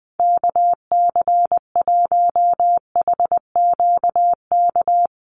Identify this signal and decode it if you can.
Morse code, heard as a keyed tone, about 20 wpm, 700 Hz. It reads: KC1HQK
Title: USA Call Signs Repeated 3x 20wpm